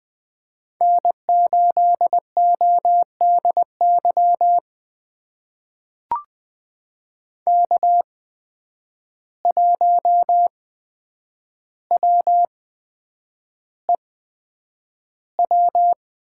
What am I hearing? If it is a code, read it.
Morse code, heard as a keyed tone, about 20 wpm, spaced as for 9 wpm, 700 Hz. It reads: N8ODY K1WEW
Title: USA Call Signs - 5 Characters - 8x Character Spacing - 20wpm